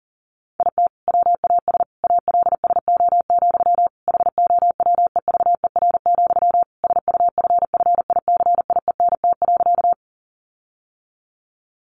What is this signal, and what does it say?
Morse code, heard as a keyed tone, about 40 wpm, 700 Hz. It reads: IT WAS ALSO, HOWEVER, SUFFICIENT.
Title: Sentences from Top 1400 Words 40wpm